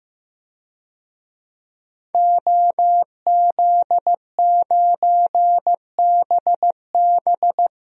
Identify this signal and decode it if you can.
Morse code, heard as a keyed tone, about 15 wpm, 700 Hz. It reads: OZ9BB
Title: DXCC-100 Call Signs - 5 Characters - 15wpm